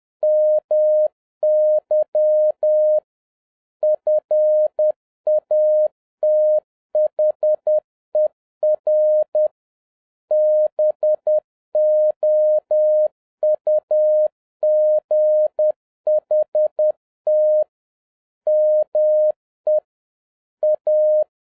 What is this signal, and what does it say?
Morse code, heard as a keyed tone, about 10 wpm, 610 Hz. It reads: MY FATHER BOUGHT ME A
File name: 2011-10-mp3 / n11-MEETING WITH GOOD FRIENDS 10wpm